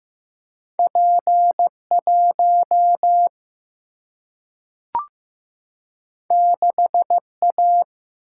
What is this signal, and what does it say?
Morse code, heard as a keyed tone, about 15 wpm, 700 Hz. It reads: P 1 6 A
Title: Single-Letter Single-Number 15wpm